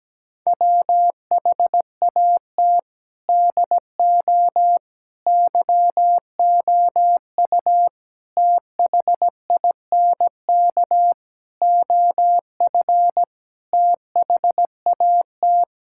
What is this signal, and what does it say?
Morse code, heard as a keyed tone, about 17 wpm, 700 Hz. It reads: WHAT DO YOU THINK OF THAT
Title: Sentences from Top 1000 Words - Review Part 6 - 17wpm